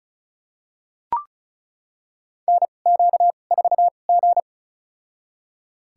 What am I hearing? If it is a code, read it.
Morse code, heard as a keyed tone, about 36 wpm, spaced as for 25 wpm, 700 Hz. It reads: NQ4G
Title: USA Call Signs - 2x Character Spacing - 35wpm